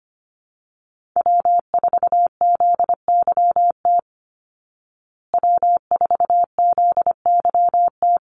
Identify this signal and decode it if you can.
Morse code, heard as a keyed tone, about 25 wpm, 700 Hz. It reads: W4ZYT W4ZYT
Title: Call Signs Repeated 3x - 25wpm